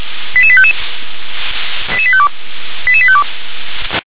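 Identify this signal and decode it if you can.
Five-tone selective call (ZVEI), heard as quick runs of five tones, tones of about 70 ms. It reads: 88855 88852 88852
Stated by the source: ZVEI five-tone selective call received on an SDR.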